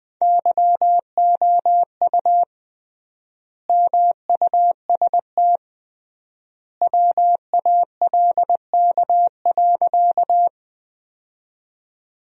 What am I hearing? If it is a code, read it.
Morse code, heard as a keyed tone, about 20 wpm, 700 Hz. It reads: YOU MUST WALK.
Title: 3x Word-Spacing - Sentences from Top 300 Words - 20wpm